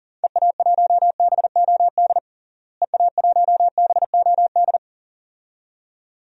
Morse code, heard as a keyed tone, about 37 wpm, 700 Hz. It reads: EA1BOD EA1BOD
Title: DXCC-100 Call Signs - 6 Characters - Repeated 2x - 40wpm